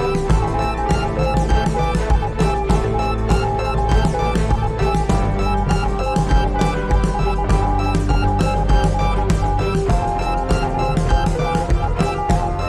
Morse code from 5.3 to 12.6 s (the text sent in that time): DOESN'T MATTER